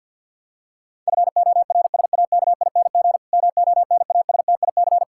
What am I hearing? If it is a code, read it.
Morse code, heard as a keyed tone, about 52 wpm, 700 Hz. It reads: FORSAKING MONASTIC
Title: Pangram Challenge v2 - 50wpm